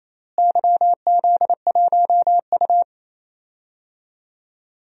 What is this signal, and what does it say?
Morse code, heard as a keyed tone, about 27 wpm, 700 Hz. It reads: YZ1U
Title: DXCC-100 Call Signs - 4 Characters - Repeated 2x - 28wpm